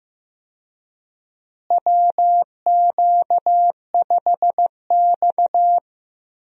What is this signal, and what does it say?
Morse code, heard as a keyed tone, about 15 wpm, 700 Hz. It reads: WQ5X
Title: USA Call Signs - 4 Characters - Repeated 2x - 15wpm